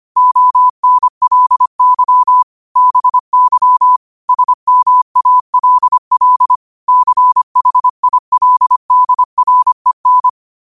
Morse code, heard as a keyed tone, about 25 wpm, 1000 Hz. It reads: ONLY BY SMALL CHILDREN
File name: qotdfiles / 25wpm